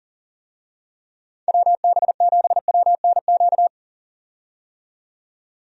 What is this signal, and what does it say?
Morse code, heard as a keyed tone, about 40 wpm, 700 Hz. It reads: WB7WNQ
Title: USA Call Signs - 6 Characters - 40wpm